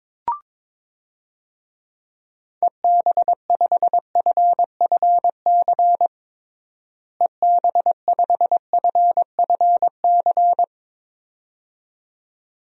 Morse code, heard as a keyed tone, about 22 wpm, 700 Hz. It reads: EB5FFC EB5FFC
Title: DXCC-100 Call Signs - 6 Characters - Repeated 2x - 22wpm